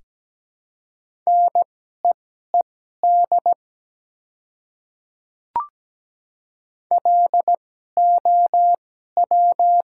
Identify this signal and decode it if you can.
Morse code, heard as a keyed tone, about 17 wpm, spaced as for 12 wpm, 700 Hz. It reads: NEED LOW
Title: 2x Character Spacing - Top 300 Words - 17wpm